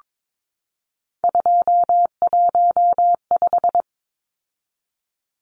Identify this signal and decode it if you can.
Morse code, heard as a keyed tone, about 22 wpm, 700 Hz. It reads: 215
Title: Numbers - 3-Digit 22wpm